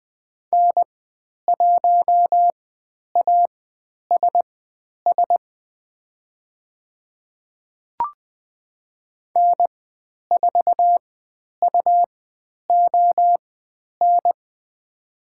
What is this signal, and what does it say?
Morse code, heard as a keyed tone, about 20 wpm, spaced as for 10 wpm, 700 Hz. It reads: N1ASS N4UON
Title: 102 Lesson 17g - Letter u - Call Signs 20wpm